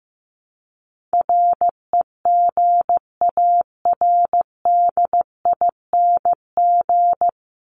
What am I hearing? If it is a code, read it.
Morse code, heard as a keyed tone, about 15 wpm, 700 Hz. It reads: REGARDING
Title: ING Suffix - Top 100 Words with Base Words 15wpm